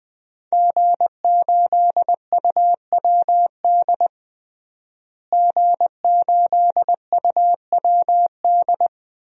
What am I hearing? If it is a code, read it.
Morse code, heard as a keyed tone, about 20 wpm, 700 Hz. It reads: G8UWD G8UWD
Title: DXCC-100 Call Signs - 5 Characters - Repeated 2x - 20wpm